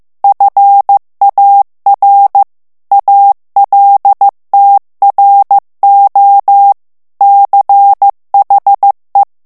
Morse code, heard as a keyed tone, about 15 wpm, 800 Hz. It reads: FAR ALTRO CHE
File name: morse mp3 / ITALIEN texte clair 2 vitesse 14 mots minute